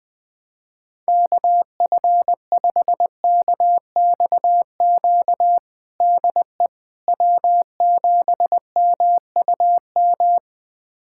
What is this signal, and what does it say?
Morse code, heard as a keyed tone, about 21 wpm, 700 Hz. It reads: KF5KXQ DE W7MUM
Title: USA Callsign DE Callsign 20wpm